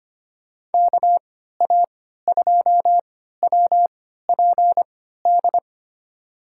Morse code, heard as a keyed tone, about 25 wpm, spaced as for 14 wpm, 700 Hz. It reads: KA2WPD
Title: USA Call Signs - 6 Characters - 3x Character Spacing - 25wpm